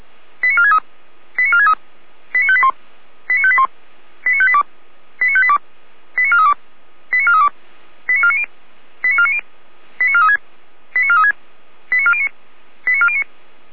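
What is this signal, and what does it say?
Five-tone selective call (ZVEI), heard as quick runs of five tones, tones of about 70 ms. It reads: 89463 89463 89581 89581 89582 89582 89432 89432 89490 89490 89436 89436 89409 89409
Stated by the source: SDR recording of a ZVEI five-tone selective call, address 89462.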